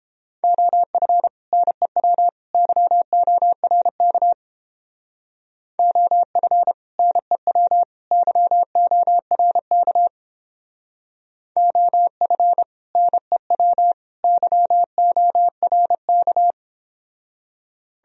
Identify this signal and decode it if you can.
Morse code, heard as a keyed tone, about 31 wpm, 700 Hz. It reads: OF NEW YORK OF NEW YORK OF NEW YORK